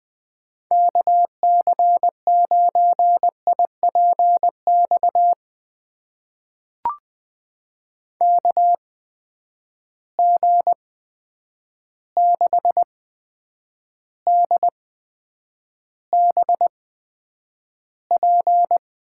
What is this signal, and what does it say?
Morse code, heard as a keyed tone, about 20 wpm, spaced as for 9 wpm, 700 Hz. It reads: KC9IPX KG6DBP
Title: USA Call Signs - 6 Characters - 8x Character Spacing - 20wpm